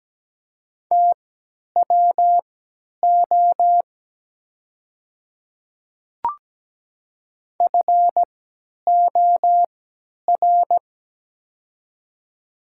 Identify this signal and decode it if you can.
Morse code, heard as a keyed tone, about 17 wpm, spaced as for 10 wpm, 700 Hz. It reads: TWO FOR
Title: 3x Character Spacing - Top 100 Words - 17wpm